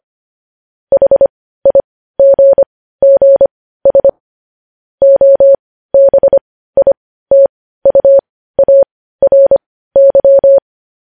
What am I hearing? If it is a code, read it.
Morse code, heard as a keyed tone, about 25 wpm, spaced as for 15 wpm, 550 Hz. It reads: HIGGS OBITUARY
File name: podcast / News headlines-1x-25x15